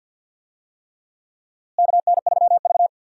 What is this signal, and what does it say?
Morse code, heard as a keyed tone, about 49 wpm, 700 Hz. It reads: KN3V
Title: USA Call Signs - 4 Characters - Repeated 3x - 50wpm